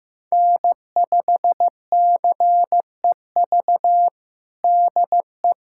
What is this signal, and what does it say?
Morse code, heard as a keyed tone, about 14 wpm, 700 Hz. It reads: N5CEV DE
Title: USA Callsign DE Callsign 15wpm